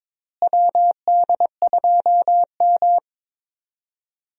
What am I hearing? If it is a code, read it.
Morse code, heard as a keyed tone, about 22 wpm, 700 Hz. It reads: WD2M
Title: USA Call Signs - 4 Characters - Repeated 3x - 22wpm